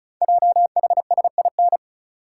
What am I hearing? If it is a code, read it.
Morse code, heard as a keyed tone, about 36 wpm, 690 Hz. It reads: JHSIN
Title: Code Groups - 5 Letters - 35wpm